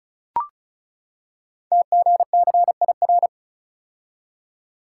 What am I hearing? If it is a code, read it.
Morse code, heard as a keyed tone, about 36 wpm, 690 Hz. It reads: TGCIR
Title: Code Groups - 5 Letters - 35wpm